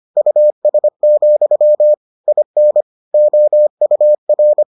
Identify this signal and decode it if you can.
Morse code, heard as a keyed tone, about 25 wpm, 600 Hz. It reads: US, IN OUR